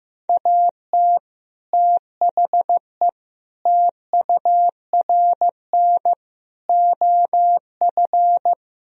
Morse code, heard as a keyed tone, about 15 wpm, 700 Hz. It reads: AT THE TURN OF
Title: Speed-Racing - Sets of 5 Words 15wpm